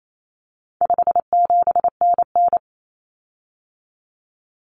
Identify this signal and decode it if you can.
Morse code, heard as a keyed tone, about 28 wpm, 700 Hz. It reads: 57NN
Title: Common Cut Numbers - 4-Digit 28wpm